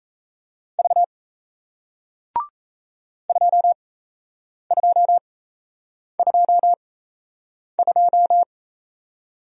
Morse code, heard as a keyed tone, about 38 wpm, spaced as for 8 wpm, 700 Hz. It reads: V 2222